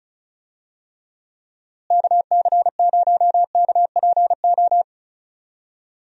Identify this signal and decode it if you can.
Morse code, heard as a keyed tone, about 36 wpm, 700 Hz. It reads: KC0KPO
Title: USA Call Signs - 6 Characters - Repeated 3x - 35wpm